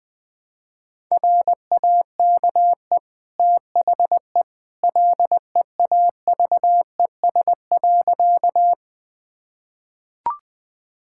Morse code, heard as a keyed tone, about 20 wpm, 700 Hz. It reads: RAKE THE LEAVES.
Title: Easy Sentences with Hints - 20wpm